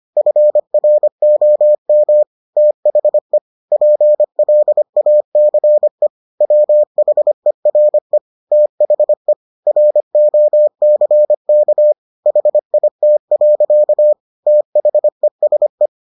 Morse code, heard as a keyed tone, about 25 wpm, 600 Hz. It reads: FROM THE PLACE WHERE THE ROCK HIT. THESE